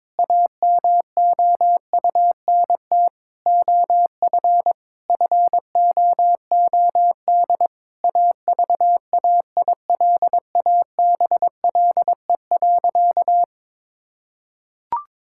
Morse code, heard as a keyed tone, about 22 wpm, 700 Hz. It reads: AMOUNT OF FOOD AVAILABLE.
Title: Sentences from Top 1000 Words - Review Part 6 - 22wpm